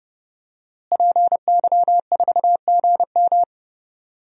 Morse code, heard as a keyed tone, about 31 wpm, 700 Hz. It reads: PY4GM
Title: DXCC-100 Call Signs - 5 Characters - Repeated 3x - 30wpm